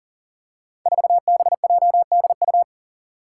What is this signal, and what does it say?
Morse code, heard as a keyed tone, about 41 wpm, 700 Hz. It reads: 4BJDU
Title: Five Challenging Letter-Number Combinations 40wpm